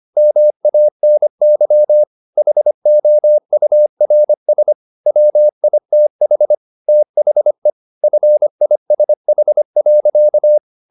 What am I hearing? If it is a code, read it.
Morse code, heard as a keyed tone, about 25 wpm, 600 Hz. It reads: MANY HOURS WITH THE FISH.
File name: the old man and the sea part 1 25wpm